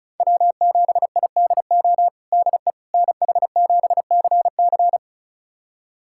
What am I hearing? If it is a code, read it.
Morse code, heard as a keyed tone, about 36 wpm, 700 Hz. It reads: W7IDO DE NH7CC
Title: USA Callsign DE Callsign - 5 Characters - 35wpm